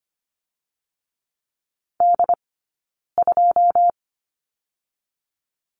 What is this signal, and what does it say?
Morse code, heard as a keyed tone, about 25 wpm, 700 Hz. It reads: D 2
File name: Lesson 17d - Letter u - Two Characters 25wpm